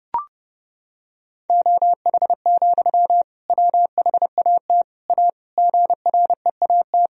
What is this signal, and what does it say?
Morse code, heard as a keyed tone, about 32 wpm, 700 Hz. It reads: OH, WHAT A GREAT
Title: Sentences from Top 1000 Words - Review Part 1 - 30wpm